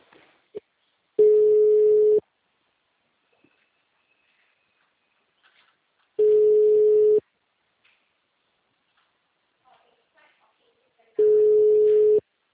Ringback tone (Europe, 425 Hz).
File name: out- -2030- - - .42334